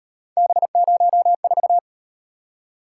Morse code, heard as a keyed tone, about 37 wpm, 700 Hz. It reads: B04